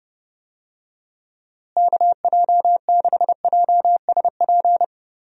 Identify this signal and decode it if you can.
Morse code, heard as a keyed tone, about 30 wpm, 700 Hz. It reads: KJ6JSP
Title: USA Call Signs - 6 Characters - Repeated 2x - 30wpm